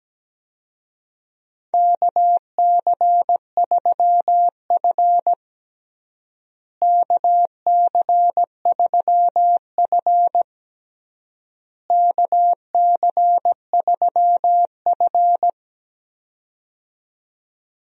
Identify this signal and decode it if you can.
Morse code, heard as a keyed tone, about 17 wpm, 700 Hz. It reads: KC3F KC3F KC3F
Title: USA Call Signs - 4 Characters - Repeated 3x - 17wpm